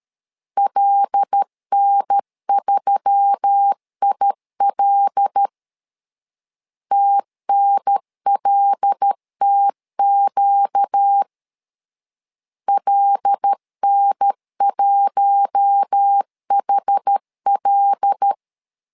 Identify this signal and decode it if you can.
Morse code, heard as a keyed tone, about 13 wpm, 800 Hz. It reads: LN3IL TNLTQ LN1HL
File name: LEZIONI / 5 LBVC / 5 L 12wpm 1913